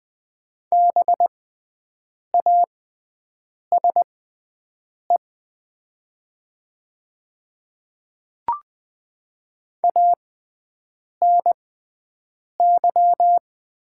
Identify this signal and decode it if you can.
Morse code, heard as a keyed tone, about 20 wpm, spaced as for 7 wpm, 700 Hz. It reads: BASE ANY
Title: 6x Character Spacing - Top 300 Words - 20wpm